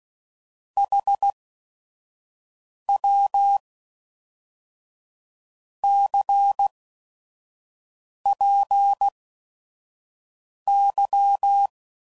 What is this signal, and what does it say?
Morse code, heard as a keyed tone, about 16 wpm, spaced as for 5 wpm, 800 Hz. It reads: HW CPY
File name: CW Code Groups @5WPM / QSOs / WD0FBT 1